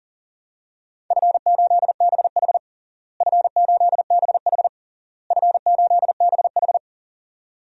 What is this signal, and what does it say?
Morse code, heard as a keyed tone, about 40 wpm, 700 Hz. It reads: F8BH F8BH F8BH
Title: DXCC-100 Call Signs - 4 Characters - Repeated 3x - 40wpm